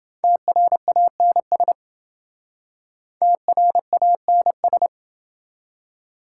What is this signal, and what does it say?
Morse code, heard as a keyed tone, about 27 wpm, 700 Hz. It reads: TRANS TRANS